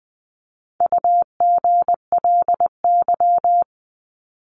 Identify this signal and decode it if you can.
Morse code, heard as a keyed tone, about 20 wpm, 700 Hz. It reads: UGLY
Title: 260 Lesson 42e - Words 20-20wpm